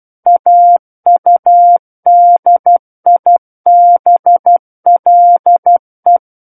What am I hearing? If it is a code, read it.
Morse code, heard as a keyed tone, about 12 wpm, 700 Hz. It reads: AUDIBLE
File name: Book One / 03-Chapter 3 - 12 WPM